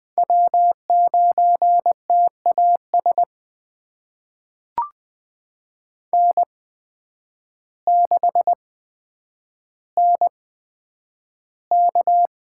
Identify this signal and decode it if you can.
Morse code, heard as a keyed tone, about 20 wpm, spaced as for 9 wpm, 700 Hz. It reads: W9TAS N6NK
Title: USA Call Signs - 8x Character Spacing - 20wpm